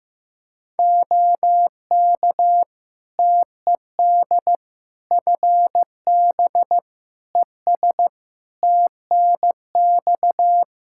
Morse code, heard as a keyed tone, about 16 wpm, 700 Hz. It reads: OK TED FB ES TNX
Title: Practice QSO - Exchange 2 - Info - 15wpm